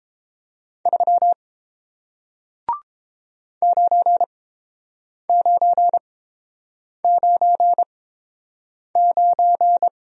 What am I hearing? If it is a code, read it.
Morse code, heard as a keyed tone, about 30 wpm, spaced as for 7 wpm, 700 Hz. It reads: 3 9999